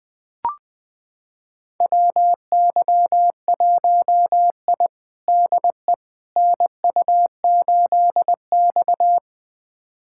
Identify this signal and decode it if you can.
Morse code, heard as a keyed tone, about 21 wpm, 700 Hz. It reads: WY1I DE NU8X
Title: USA Callsign DE Callsign - 4 Characters - 20wpm